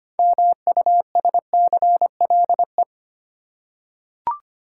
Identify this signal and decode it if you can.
Morse code, heard as a keyed tone, about 25 wpm, 700 Hz. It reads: MUSCLE
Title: Six Letter Words - 25wpm